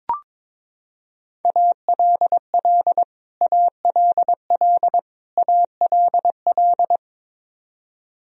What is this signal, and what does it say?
Morse code, heard as a keyed tone, about 22 wpm, 700 Hz. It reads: ALL ALL ALL
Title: Top 100 Words Repeated 22wpm